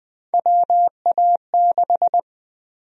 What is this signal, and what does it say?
Morse code, heard as a keyed tone, about 20 wpm, 700 Hz. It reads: WA6
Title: USA Call Sign Prefixes - 20wpm